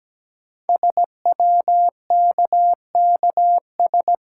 Morse code, heard as a keyed tone, about 17 wpm, 700 Hz. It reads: SWKKS
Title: Code Groups - 5 Letters - 17wpm